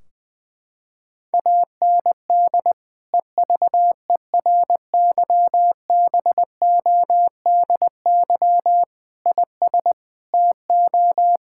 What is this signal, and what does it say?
Morse code, heard as a keyed tone, about 21 wpm, 700 Hz. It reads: AND EVERYBODY IS TO
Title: Sentences from Top 1600 Words 20wpm